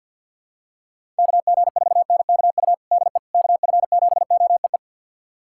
Morse code, heard as a keyed tone, about 45 wpm, 700 Hz. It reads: KG4NKU DE KF7OEE
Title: USA Callsign DE Callsign - 6 Characters - 50wpm